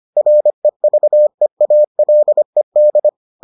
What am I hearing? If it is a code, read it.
Morse code, heard as a keyed tone, about 25 wpm, 600 Hz. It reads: REVEALED